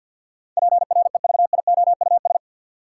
Morse code, heard as a keyed tone, about 53 wpm, 700 Hz. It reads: PREVIOUS
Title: Sentences from Top 1500 Words - New Word Drill - 50wpm